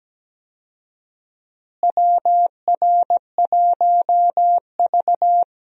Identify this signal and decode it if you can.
Morse code, heard as a keyed tone, about 17 wpm, 700 Hz. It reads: WR1V
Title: USA Call Signs - 4 Characters - Repeated 2x - 17wpm